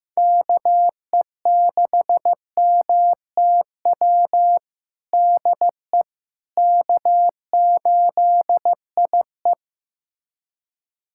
Morse code, heard as a keyed tone, about 15 wpm, 700 Hz. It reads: KE6MTW DE K8IE
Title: USA Callsign DE Callsign 15wpm